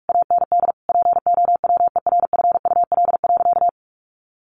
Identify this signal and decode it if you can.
Morse code, heard as a keyed tone, about 44 wpm, 700 Hz. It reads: AND POWERFUL.
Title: Sentences from Top 1300 Words 45wpm